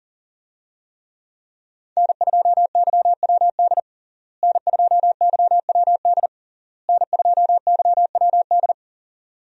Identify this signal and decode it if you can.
Morse code, heard as a keyed tone, about 37 wpm, 700 Hz. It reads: N2YWD N2YWD N2YWD
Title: USA Call Signs - 5 Characters - Repeated 3x - 40wpm